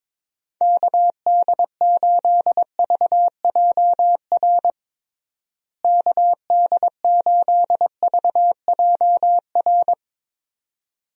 Morse code, heard as a keyed tone, about 23 wpm, 700 Hz. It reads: KD8VJR KD8VJR
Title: USA Call Signs - 6 Characters - Repeated 2x - 22wpm